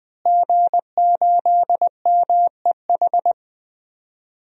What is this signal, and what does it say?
Morse code, heard as a keyed tone, about 21 wpm, 700 Hz. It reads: G8MEH
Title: DXCC-100 Call Signs - 5 Characters - Repeated 3x - 20wpm